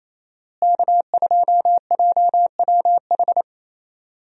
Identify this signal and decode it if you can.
Morse code, heard as a keyed tone, about 28 wpm, 700 Hz. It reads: K2JWH
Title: USA Call Signs - 5 Characters - Repeated 2x - 28wpm